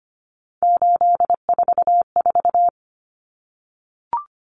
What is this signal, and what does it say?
Morse code, heard as a keyed tone, about 25 wpm, 700 Hz. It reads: 844
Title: Numbers - 3-Digit 25wpm